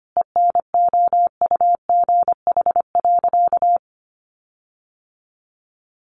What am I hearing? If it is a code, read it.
Morse code, heard as a keyed tone, about 25 wpm, 700 Hz. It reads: ENOUGH.
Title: Sentences from Top 1500 Words 25wpm